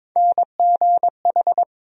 Morse code, heard as a keyed tone, about 22 wpm, 700 Hz. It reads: NGH
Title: USA Call Sign Postfixes - 3 Letters 22wpm